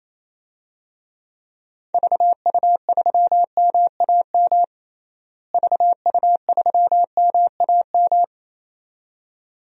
Morse code, heard as a keyed tone, about 28 wpm, 700 Hz. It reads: VU3MAM VU3MAM
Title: DXCC-100 Call Signs - 6 Characters - Repeated 2x - 28wpm